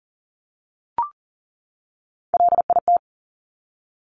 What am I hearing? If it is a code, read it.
Morse code, heard as a keyed tone, about 42 wpm, 690 Hz. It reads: LIT
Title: Top Three Letter Combinations in Words 40wpm